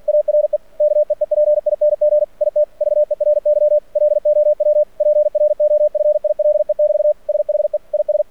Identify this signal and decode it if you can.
Morse code, heard as a keyed tone, about 49 wpm, 600 Hz. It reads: ARE KEEPING IT VERY LOW PROFILE <BT> SHE IS
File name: archived / CRSnet